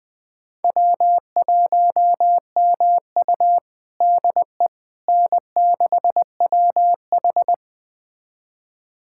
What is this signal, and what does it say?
Morse code, heard as a keyed tone, about 20 wpm, 700 Hz. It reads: W1MU DE N6WH
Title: USA Callsign DE Callsign - 4 Characters - 20wpm